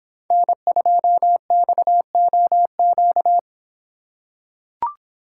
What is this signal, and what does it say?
Morse code, heard as a keyed tone, about 26 wpm, 700 Hz. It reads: N2XOQ